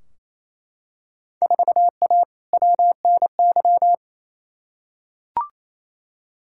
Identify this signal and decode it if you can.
Morse code, heard as a keyed tone, about 28 wpm, 700 Hz. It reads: 4A WNY
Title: Field Day Exchanges Encore - 28wpm